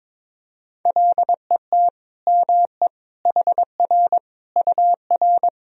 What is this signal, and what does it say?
Morse code, heard as a keyed tone, about 21 wpm, 700 Hz. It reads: LET ME HR UR
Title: Sentences from Top 300 Words and Abbreviations 22wpm